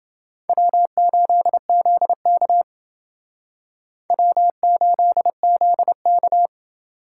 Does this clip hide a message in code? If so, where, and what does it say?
Yes: Morse code from 0.5 to 6.5 s, heard as a keyed tone, about 28 wpm, 700 Hz. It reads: W8ZK W8ZK